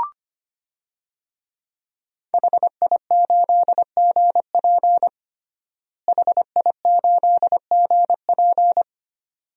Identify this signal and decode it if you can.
Morse code, heard as a keyed tone, about 25 wpm, 700 Hz. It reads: HI8GP HI8GP
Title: DXCC-100 Call Signs - 5 Characters - Repeated 2x - 25wpm